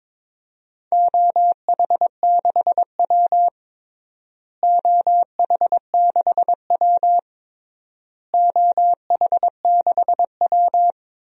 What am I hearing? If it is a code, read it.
Morse code, heard as a keyed tone, about 22 wpm, 700 Hz. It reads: OH6W OH6W OH6W
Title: DXCC-100 Call Signs - 4 Characters - Repeated 3x - 22wpm